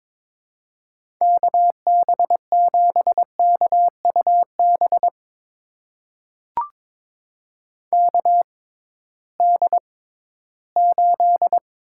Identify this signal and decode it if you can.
Morse code, heard as a keyed tone, about 22 wpm, spaced as for 14 wpm, 700 Hz. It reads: KB7KUB KD8
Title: USA Call Signs - 6 Characters - 6x Character Spacing - 22wpm